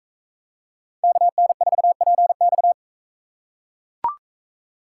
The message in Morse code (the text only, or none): KN4PX